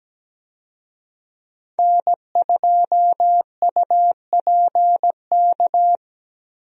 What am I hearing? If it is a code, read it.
Morse code, heard as a keyed tone, about 17 wpm, 700 Hz. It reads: N2UPK
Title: USA Call Signs - 5 Characters - Repeated 3x - 17wpm